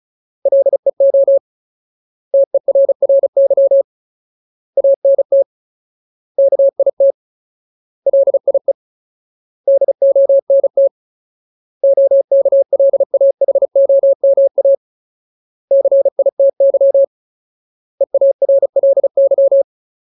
Morse code, heard as a keyed tone, about 34 wpm, 550 Hz. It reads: LEO TERRY ANT KIT LIE DONT OKLAHOMA CITY EARLY
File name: Ultimate Word List / 35WPM Ultimate